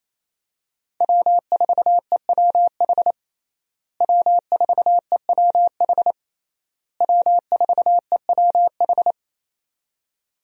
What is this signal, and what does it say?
Morse code, heard as a keyed tone, about 28 wpm, 700 Hz. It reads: W4EWH W4EWH W4EWH
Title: USA Call Signs - 5 Characters - Repeated 3x - 28wpm